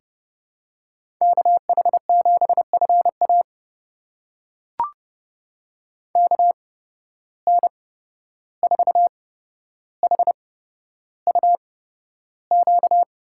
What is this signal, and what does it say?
Morse code, heard as a keyed tone, about 30 wpm, spaced as for 12 wpm, 700 Hz. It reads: KH7FA KN4HUQ
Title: USA Call Signs - 8x Character Spacing - 30wpm